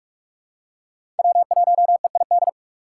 Morse code, heard as a keyed tone, about 45 wpm, 700 Hz. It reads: W1EID
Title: USA Call Signs - 5 Characters - 6x Character Spacing - 45wpm